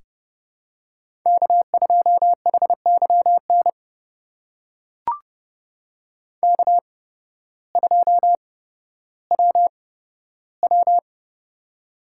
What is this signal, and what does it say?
Morse code, heard as a keyed tone, about 32 wpm, spaced as for 14 wpm, 700 Hz. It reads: K2HYN K2WW
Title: USA Call Signs - 8x Character Spacing - 30wpm